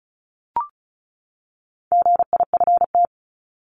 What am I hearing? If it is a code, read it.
Morse code, heard as a keyed tone, about 34 wpm, 700 Hz. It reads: GIFT
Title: Top 751-1000 Words 35wpm